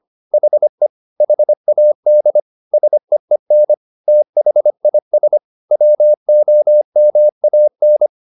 Morse code, heard as a keyed tone, about 25 wpm, 600 Hz. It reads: HE HAD SEEN THIS WOMAN